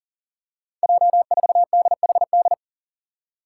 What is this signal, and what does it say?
Morse code, heard as a keyed tone, about 40 wpm, 690 Hz. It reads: J4DHD
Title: Five Challenging Letter-Number Combinations 40wpm